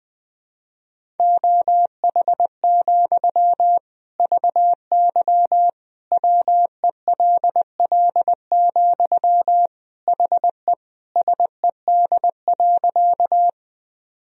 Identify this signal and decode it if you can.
Morse code, heard as a keyed tone, about 20 wpm, 700 Hz. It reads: OH, VY WELL, HE SED.
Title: Sentences from Top 500 Words and Abbreviations 20wpm